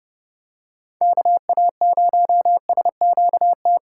Morse code, heard as a keyed tone, about 30 wpm, 700 Hz. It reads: KA0SQT
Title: USA Call Signs - 6 Characters - 2x Character Spacing - 30wpm